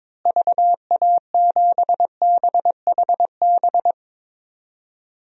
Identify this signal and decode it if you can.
Morse code, heard as a keyed tone, about 22 wpm, 700 Hz. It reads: VA7BHB
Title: DXCC-100 Call Signs - 6 Characters - Repeated 3x - 22wpm